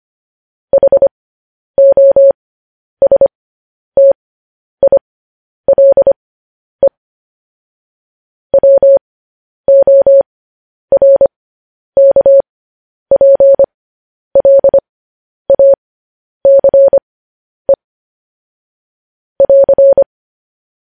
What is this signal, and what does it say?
Morse code, heard as a keyed tone, about 25 wpm, spaced as for 10 wpm, 550 Hz. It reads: HOSTILE WORKPLACE <AR>
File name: podcast / News headlines-1x-25x10